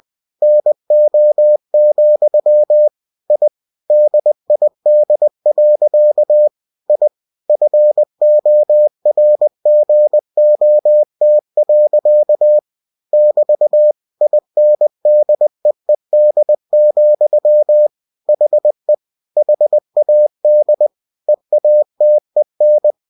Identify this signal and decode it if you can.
Morse code, heard as a keyed tone, about 20 wpm, 600 Hz. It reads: NO, I DID. I FORGOT. <BT> INDEED, HE HAD EATEN
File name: cw-books / winnie-the-pooh / winnie-20wpm-page-0040